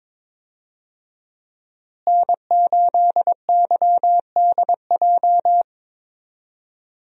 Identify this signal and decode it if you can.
Morse code, heard as a keyed tone, about 22 wpm, 700 Hz. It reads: N8YDJ
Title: USA Call Signs - 5 Characters - Repeated 3x - 22wpm